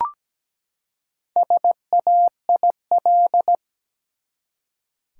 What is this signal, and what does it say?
Morse code, heard as a keyed tone, about 17 wpm, 700 Hz. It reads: SAIL
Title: Four Letter Words - 17wpm